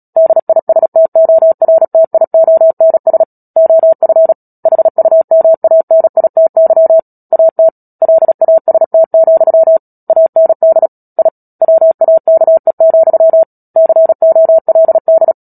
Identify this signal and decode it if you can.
Morse code, heard as a keyed tone, about 37 wpm, 650 Hz. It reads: DISTORTIONS OF HUMANITY AT LAST, AND I WAKE, COLD
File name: Book Two / 10-Chapter 10 - Epilogue - 36 WPM